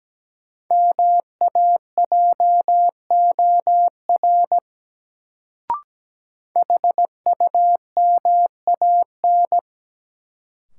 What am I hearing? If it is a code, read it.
Morse code, heard as a keyed tone, about 17 wpm, 700 Hz. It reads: MAJOR HUMAN
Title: Sentences from Top 600 Words - New Word Drill - 17wpm